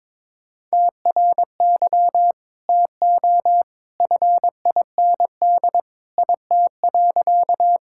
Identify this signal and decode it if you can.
Morse code, heard as a keyed tone, about 22 wpm, 700 Hz. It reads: TRY TO FIND IT.
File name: practice / 22wpm / Sentences with 2-Word Phrases - 22wpm